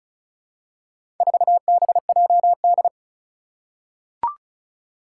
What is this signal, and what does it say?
Morse code, heard as a keyed tone, about 35 wpm, 700 Hz. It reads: 4BJD
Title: Four Challenging Letter-Number Combinations 35wpm